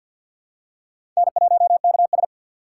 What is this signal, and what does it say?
Morse code, heard as a keyed tone, about 50 wpm, 700 Hz. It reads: N1KS
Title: USA Call Signs - 4 Characters - 6x Character Spacing - 50wpm